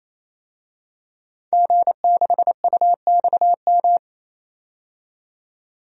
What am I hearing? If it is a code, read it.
Morse code, heard as a keyed tone, about 28 wpm, 700 Hz. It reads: G6UXM
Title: DXCC-100 Call Signs - 5 Characters - 28wpm